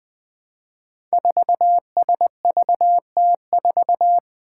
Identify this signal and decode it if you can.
Morse code, heard as a keyed tone, about 20 wpm, 700 Hz. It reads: 4SVT4
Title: Five Challenging Letter-Number Combinations 20wpm